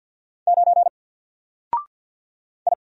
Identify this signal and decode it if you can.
Morse code, heard as a keyed tone, about 50 wpm, 700 Hz. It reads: 9 I
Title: Single Letter-Number Alt 50wpm